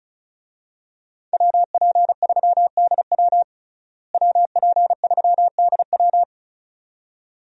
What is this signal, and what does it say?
Morse code, heard as a keyed tone, about 34 wpm, 700 Hz. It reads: WP3DW WP3DW
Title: USA Call Signs - 5 Characters - Repeated 2x - 35wpm